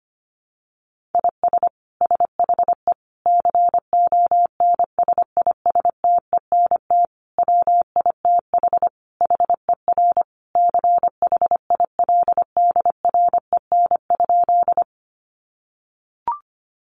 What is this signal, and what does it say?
Morse code, heard as a keyed tone, about 25 wpm, 700 Hz. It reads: IS SHE CONSISTENT WITH HER CHILDREN?
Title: Sentences from Top 1500 Words 25wpm